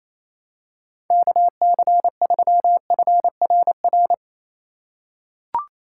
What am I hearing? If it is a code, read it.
Morse code, heard as a keyed tone, about 27 wpm, 700 Hz. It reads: KC3FRR
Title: USA Call Signs - 3x Character Spacing - 28wpm